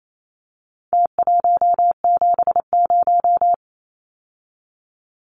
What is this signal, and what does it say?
Morse code, heard as a keyed tone, about 28 wpm, 700 Hz. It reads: T170
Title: Numbers - Contest Style - 4-Digit 28wpm